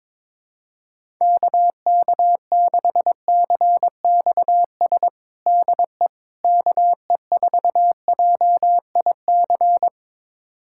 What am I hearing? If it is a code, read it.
Morse code, heard as a keyed tone, about 22 wpm, 700 Hz. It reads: KK6CXS DE KE4JIC
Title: USA Callsign DE Callsign - 6 Characters - 22wpm